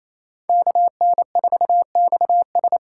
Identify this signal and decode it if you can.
Morse code, heard as a keyed tone, about 28 wpm, 700 Hz. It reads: KN4XS
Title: USA Call Signs - 5 Characters - 6x Character Spacing - 28wpm